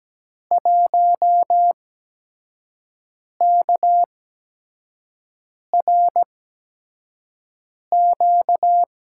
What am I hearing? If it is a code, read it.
Morse code, heard as a keyed tone, about 17 wpm, spaced as for 5 wpm, 700 Hz. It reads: 1KRQ
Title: USA Call Signs - 5 Characters - 8x Character Spacing - 17wpm